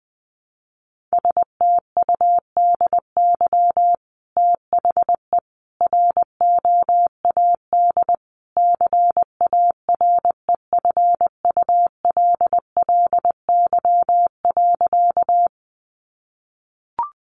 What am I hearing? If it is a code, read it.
Morse code, heard as a keyed tone, about 20 wpm, 700 Hz. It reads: STUDY THE ROAD CAREFULLY.
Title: Sentences from Top 1300 Words 20wpm